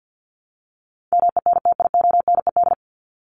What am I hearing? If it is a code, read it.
Morse code, heard as a keyed tone, about 49 wpm, 700 Hz. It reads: MENTIONED
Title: ED Suffix - Top 100 Words with Base Words 50wpm